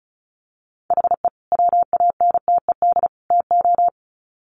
Morse code, heard as a keyed tone, about 34 wpm, 700 Hz. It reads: HE WANTED TO
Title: Sets of 3 Words 35wpm